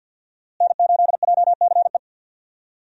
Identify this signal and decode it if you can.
Morse code, heard as a keyed tone, about 51 wpm, 700 Hz. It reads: N8JCE
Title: USA Call Signs - 5 Characters - Repeated 3x - 50wpm